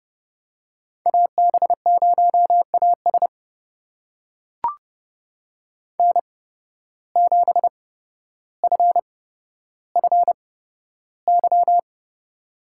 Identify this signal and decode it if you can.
Morse code, heard as a keyed tone, about 30 wpm, spaced as for 13 wpm, 700 Hz. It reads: AB0AS N7FFY
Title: USA Call Signs - 5 Characters - 8x Character Spacing - 30wpm